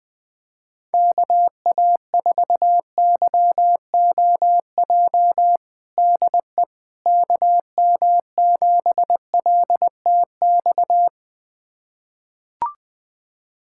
Morse code, heard as a keyed tone, about 20 wpm, 700 Hz. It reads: KA4YOJ DE KM7LTX
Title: USA Callsign DE Callsign 20wpm